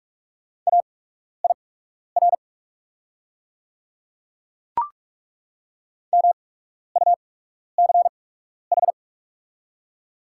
Morse code, heard as a keyed tone, about 43 wpm, spaced as for 12 wpm, 700 Hz. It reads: AIR MUCH
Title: 8x Character Spacing - Top 300 Words - 45wpm